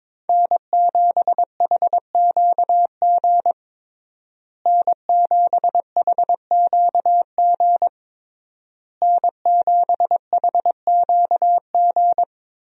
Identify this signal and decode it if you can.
Morse code, heard as a keyed tone, about 22 wpm, 700 Hz. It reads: N7HQG N7HQG N7HQG
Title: USA Call Signs - 5 Characters - Repeated 3x - 22wpm